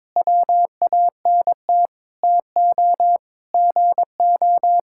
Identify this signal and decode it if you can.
Morse code, heard as a keyed tone, about 22 wpm, 700 Hz. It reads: WANT TO GO
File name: practice / 22wpm / Sentences with 4-Word Phrases - 22wpm